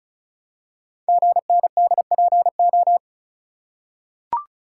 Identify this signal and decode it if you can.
Morse code, heard as a keyed tone, about 36 wpm, 690 Hz. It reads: GNDPO
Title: Code Groups - 5 Letters - 35wpm